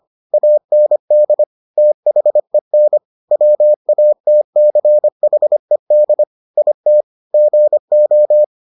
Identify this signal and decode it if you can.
Morse code, heard as a keyed tone, about 25 wpm, 600 Hz. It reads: AND THEN WATCHED IT GO